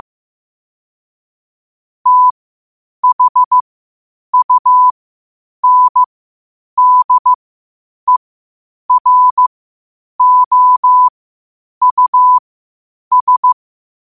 Morse code, heard as a keyed tone, about 15 wpm, spaced as for 8 wpm, 1000 Hz. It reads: THUNDEROUS
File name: qotdfiles / 8wpm